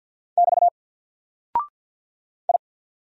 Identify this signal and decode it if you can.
Morse code, heard as a keyed tone, about 49 wpm, 700 Hz. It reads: <BT> I
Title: Single Character (Letter:Number:Punctuation:Prosign) 50wpm